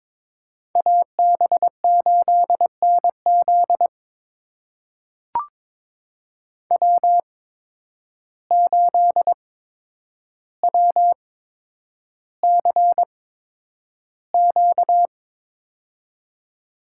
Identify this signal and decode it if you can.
Morse code, heard as a keyed tone, about 22 wpm, spaced as for 9 wpm, 700 Hz. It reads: AB8NZ W8WCQ
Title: USA Call Signs - 5 Characters - 8x Character Spacing - 22wpm